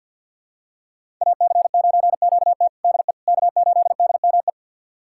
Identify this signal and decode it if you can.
Morse code, heard as a keyed tone, about 52 wpm, 700 Hz. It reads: AC9QT DE K8DME
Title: USA Callsign DE Callsign - 5 Characters - 50wpm